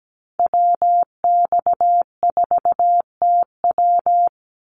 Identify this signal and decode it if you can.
Morse code, heard as a keyed tone, about 17 wpm, 700 Hz. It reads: WX4TW
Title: NA SOTA Activator Callsigns 17wpm